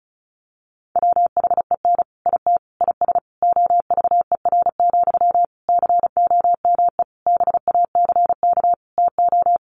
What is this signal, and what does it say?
Morse code, heard as a keyed tone, about 34 wpm, 700 Hz. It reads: WHEN IT IS OVER, COME BACK TO
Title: Sentences from Top 1200 Words 35wpm